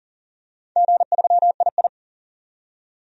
Morse code, heard as a keyed tone, about 40 wpm, 700 Hz. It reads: G3II
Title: DXCC-100 Call Signs - 4 Characters - 40wpm